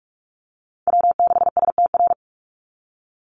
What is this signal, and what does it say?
Morse code, heard as a keyed tone, about 44 wpm, 690 Hz. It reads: W6STR
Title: NA SOTA Chaser Callsigns 45wpm